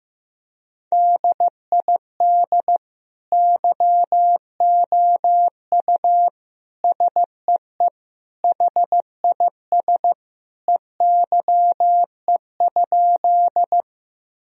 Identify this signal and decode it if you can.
Morse code, heard as a keyed tone, about 15 wpm, 700 Hz. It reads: DID YOU SEE HIS EYE?
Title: Sentences from Top 1000 Words - Review Part 1 - 15wpm